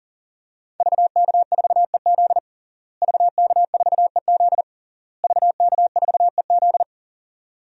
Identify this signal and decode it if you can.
Morse code, heard as a keyed tone, about 40 wpm, 700 Hz. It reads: VK4EZ VK4EZ VK4EZ
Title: DXCC-100 Call Signs - 5 Characters - Repeated 3x - 40wpm